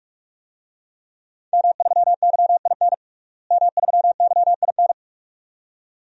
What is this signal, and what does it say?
Morse code, heard as a keyed tone, about 46 wpm, 700 Hz. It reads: M3YIN M3YIN
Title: DXCC-100 Call Signs - 5 Characters - Repeated 2x - 45wpm